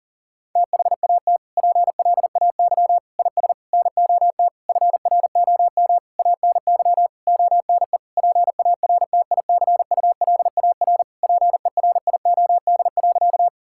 Morse code, heard as a keyed tone, about 40 wpm, 700 Hz. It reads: THAT PLAY IS NOT FROM ANY ONE PARTICULAR PERIOD.
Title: Sentences from Top 1000 Words - Review Part 5 - 40wpm